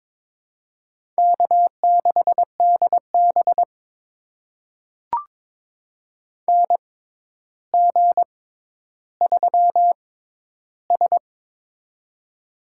Morse code, heard as a keyed tone, about 22 wpm, spaced as for 11 wpm, 700 Hz. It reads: K6DB NG3S
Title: USA Call Signs - 4 Characters - 6x Character Spacing - 22wpm